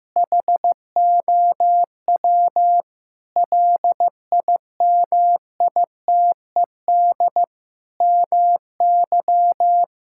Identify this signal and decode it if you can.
Morse code, heard as a keyed tone, about 15 wpm, 700 Hz. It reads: HOW LIMITED MY
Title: Sentences from Top 1000 Words - Review Part 6 - 15wpm